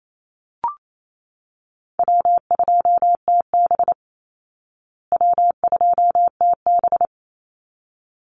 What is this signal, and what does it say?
Morse code, heard as a keyed tone, about 28 wpm, 700 Hz. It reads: W2TB W2TB
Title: Easy Call Signs Repeated 2x - 28wpm